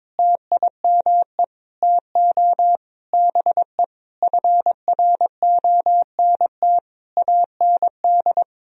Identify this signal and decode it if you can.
Morse code, heard as a keyed tone, about 22 wpm, 700 Hz. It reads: TIME TO BE FRONT AND
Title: Sentences from Top 1000 Words - Review Part 2 - 22wpm